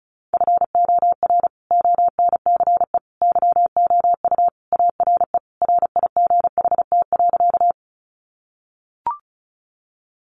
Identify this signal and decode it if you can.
Morse code, heard as a keyed tone, about 34 wpm, 700 Hz. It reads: FOR ONCE YOU ARE RIGHT.
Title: Sentences from Top 300 Words 35wpm